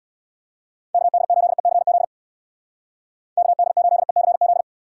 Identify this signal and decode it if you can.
Morse code, heard as a keyed tone, about 65 wpm, 700 Hz. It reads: KD8ECZ KD8ECZ